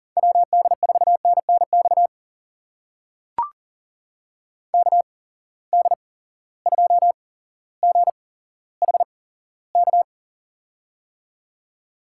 Morse code, heard as a keyed tone, about 43 wpm, spaced as for 17 wpm, 700 Hz. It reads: WD4NNX KD2GHK
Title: USA Call Signs - 6 Characters - 8x Character Spacing - 40wpm